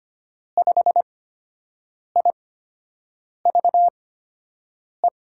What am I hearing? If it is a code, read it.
Morse code, heard as a keyed tone, about 25 wpm, spaced as for 7 wpm, 700 Hz. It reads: 5IVE
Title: USA Call Signs - 6 Characters - 8x Character Spacing - 25wpm